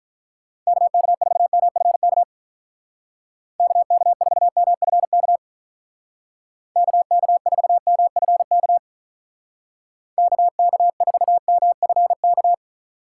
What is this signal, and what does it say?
Morse code, heard as a keyed tone, about 44 wpm, 700 Hz. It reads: KK4MFK KK4MFK KK4MFK KK4MFK
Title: Speed Racing - USA 2x3 Call Signs 35wpm